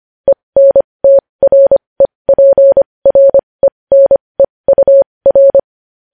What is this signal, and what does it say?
Morse code, heard as a keyed tone, about 25 wpm, spaced as for 20 wpm, 550 Hz. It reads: ENTREPRENEUR
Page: news headlines in morse code at 20 wpm